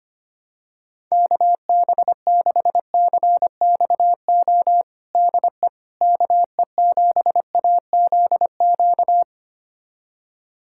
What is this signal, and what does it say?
Morse code, heard as a keyed tone, about 25 wpm, 700 Hz. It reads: KB6CXO DE KE7AZQ
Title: USA Callsign DE Callsign - 6 Characters - 25wpm